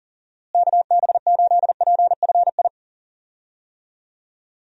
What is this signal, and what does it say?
Morse code, heard as a keyed tone, about 40 wpm, 700 Hz. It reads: KB8PFI
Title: USA Call Signs - 6 Characters - Repeated 2x - 40wpm